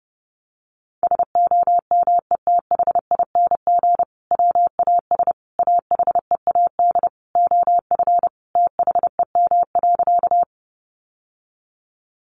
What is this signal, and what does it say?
Morse code, heard as a keyed tone, about 30 wpm, 700 Hz. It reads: SOMETHING WAS AHEAD OF THEM.
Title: Sentences from Top 1100 Words 30wpm